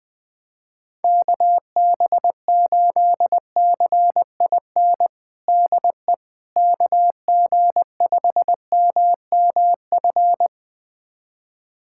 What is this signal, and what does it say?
Morse code, heard as a keyed tone, about 21 wpm, 700 Hz. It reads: KB8CIN DE KG5MMF
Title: USA Callsign DE Callsign - 6 Characters - 20wpm